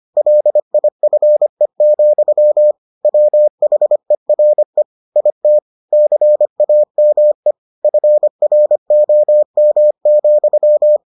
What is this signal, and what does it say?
Morse code, heard as a keyed tone, about 25 wpm, 600 Hz. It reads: LIFE, WHERE IT CAME FROM,